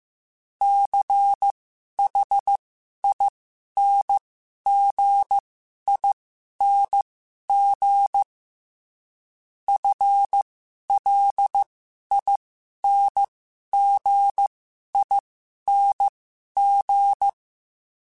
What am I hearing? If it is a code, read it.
Morse code, heard as a keyed tone, about 15 wpm, spaced as for 11 wpm, 800 Hz. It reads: CHINGING FLINGING
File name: Special INGs / Special INGs-mp3s / Dble ings @10